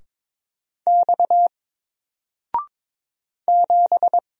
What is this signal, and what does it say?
Morse code, heard as a keyed tone, about 22 wpm, 700 Hz. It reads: X 7
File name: Single-Letter-Number-22wpm